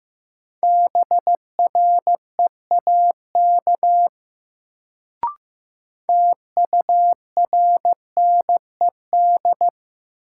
Morse code, heard as a keyed tone, about 15 wpm, 700 Hz. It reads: BREAK TURNED
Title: Sets of 4 Words - New Word Drill - 15wpm